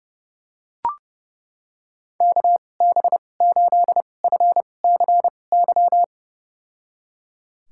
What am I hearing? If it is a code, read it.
Morse code, heard as a keyed tone, about 30 wpm, spaced as for 22 wpm, 700 Hz. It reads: KB8FCY
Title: USA Call Signs - 2x Character Spacing - 30wpm